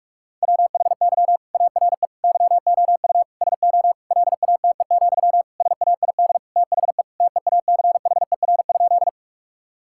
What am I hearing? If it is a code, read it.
Morse code, heard as a keyed tone, about 46 wpm, 700 Hz. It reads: WHY ARE YOU SO LATE, SAID THE TEACHER?
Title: Sentences from Top 1000 Words - Review Part 6 - 45wpm